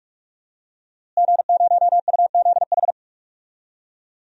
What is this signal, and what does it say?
Morse code, heard as a keyed tone, about 46 wpm, 700 Hz. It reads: G0UZH
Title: DXCC-100 Call Signs - 5 Characters - Repeated 3x - 45wpm